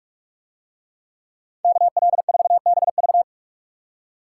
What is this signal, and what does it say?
Morse code, heard as a keyed tone, about 39 wpm, 700 Hz. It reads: KL4BV